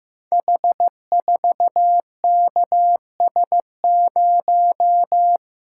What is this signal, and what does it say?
Morse code, heard as a keyed tone, about 15 wpm, 700 Hz. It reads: H4KS0
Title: Code Groups - 5 Letter:Numbers - 15wpm